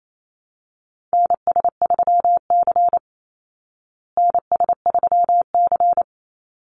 Morse code, heard as a keyed tone, about 28 wpm, 700 Hz. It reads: NS3C NS3C
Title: Call Signs Repeated 2x - 28wpm